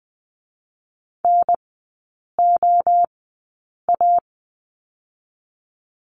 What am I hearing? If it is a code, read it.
Morse code, heard as a keyed tone, about 20 wpm, 700 Hz. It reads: N O A
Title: 023 Lesson 5e - Letter o - Three Characters 20wpm